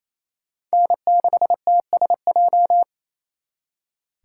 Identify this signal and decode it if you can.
Morse code, heard as a keyed tone, about 35 wpm, 700 Hz. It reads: N6TSJ
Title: Speed Racing - USA 1x3 Call Signs 28wpm